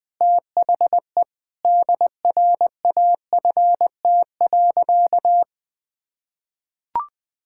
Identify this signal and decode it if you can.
Morse code, heard as a keyed tone, about 19 wpm, 700 Hz. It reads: THE DRAFT.
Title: Sentences with 6-Word Phrases - Spoken Beginning - 20wpm